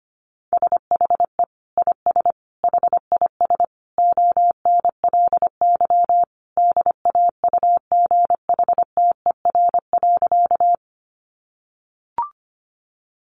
Morse code, heard as a keyed tone, about 25 wpm, 700 Hz. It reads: SHE IS HIS ONLY DAUGHTER.
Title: Sentences from Top 1500 Words 25wpm